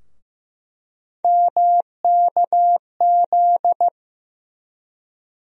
Morse code, heard as a keyed tone, about 15 wpm, 700 Hz. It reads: MKZ
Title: USA Call Sign Postfixes - 3 Letters 15wpm